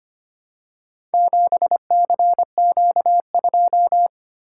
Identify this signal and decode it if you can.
Morse code, heard as a keyed tone, about 25 wpm, 700 Hz. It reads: 7CQ2
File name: Four Letter-Number Combinations - 25wpm